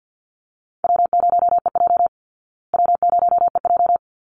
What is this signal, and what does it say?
Morse code, heard as a keyed tone, about 50 wpm, 700 Hz. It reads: W0EJ W0EJ
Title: Easy Call Signs Repeated 2x - 50wpm